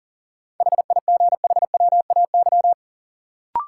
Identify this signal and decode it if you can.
Morse code, heard as a keyed tone, about 40 wpm, 700 Hz. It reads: HIGHWAY
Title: Sentences from Top 1400 Words - New Word Drill Alt - 40wpm